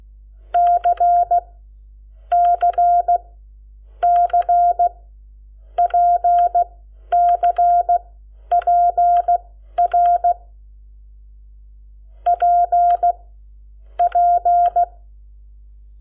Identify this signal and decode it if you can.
Morse code, heard as a keyed tone, about 16 wpm, spaced as for 11 wpm, 700 Hz. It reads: C C C PCPR P P
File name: Lecciones / CLASE Nº7